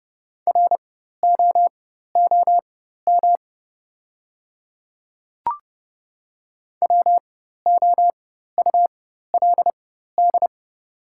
Morse code, heard as a keyed tone, about 30 wpm, spaced as for 14 wpm, 700 Hz. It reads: ROOM WOULD
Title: 4x Character Spacing - Top 300 Words - 30wpm